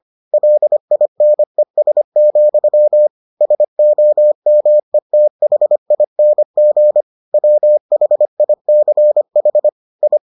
Morse code, heard as a keyed tone, about 25 wpm, 600 Hz. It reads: LINES, SOMETHING WHICH I